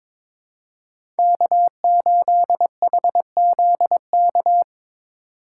Morse code, heard as a keyed tone, about 22 wpm, 700 Hz. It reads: K8HZK
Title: USA Call Signs - 5 Characters - 2x Character Spacing - 22wpm